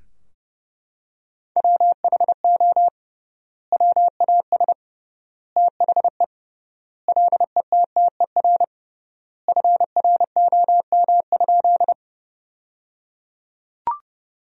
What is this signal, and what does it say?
Morse code, heard as a keyed tone, about 30 wpm, 700 Hz. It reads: WHO WAS THE LETTER FROM?
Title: 3x Word-Spacing - Sentences from Top 300 Words - 30wpm